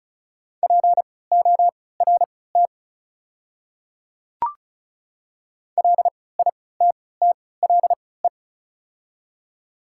Morse code, heard as a keyed tone, about 34 wpm, spaced as for 20 wpm, 690 Hz. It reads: PORT LITTLE
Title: 3x Character Spacing - Top 500 Words - 35wpm